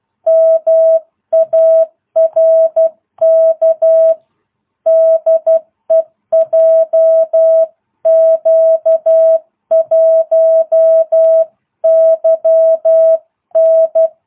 Morse code, heard as a keyed tone, about 12 wpm, 650 Hz. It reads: MARK DE JQ1YN